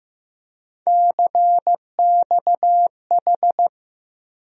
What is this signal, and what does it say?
Morse code, heard as a keyed tone, about 15 wpm, 700 Hz. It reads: CXH
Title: USA Call Sign Postfixes - 3 Letters 15wpm